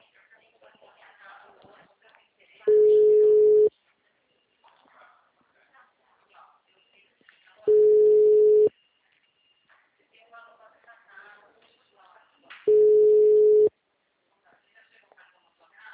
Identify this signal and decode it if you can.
Ringback tone (Europe, 425 Hz), with speech.
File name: out- -2014- - - .69598